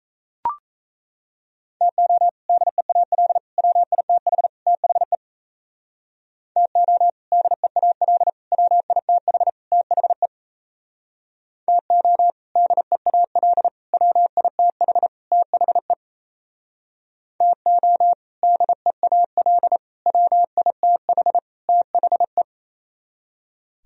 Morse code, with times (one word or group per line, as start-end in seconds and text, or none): 1.8-2.3 TO
2.5-3.4 DEAL
3.6-4.5 WITH
4.7-5.2 THE
6.6-7.1 TO
7.3-8.3 DEAL
8.5-9.5 WITH
9.7-10.3 THE
11.7-12.3 TO
12.6-13.7 DEAL
13.9-15.1 WITH
15.3-15.9 THE
17.4-18.1 TO
18.4-19.8 DEAL
20.1-21.4 WITH
21.7-22.4 THE